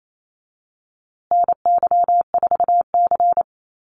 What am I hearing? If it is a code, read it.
Morse code, heard as a keyed tone, about 28 wpm, 700 Hz. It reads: NY4C
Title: USA Call Signs Repeated 3x 28wpm